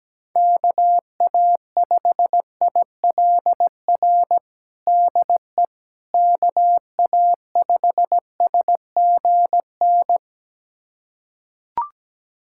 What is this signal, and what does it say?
Morse code, heard as a keyed tone, about 17 wpm, 700 Hz. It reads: KA5ILR DE KA5SGN
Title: USA Callsign DE Callsign - 6 Characters - 17wpm